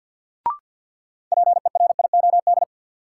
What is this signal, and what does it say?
Morse code, heard as a keyed tone, about 53 wpm, 700 Hz. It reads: PERIOD
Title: Sentences from Top 800 Words - New Word Drill - 50wpm